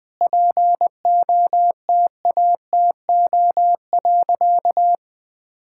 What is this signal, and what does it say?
Morse code, heard as a keyed tone, about 21 wpm, 700 Hz. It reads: POTATO.
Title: All-Age Appropriate Jokes - 20wpm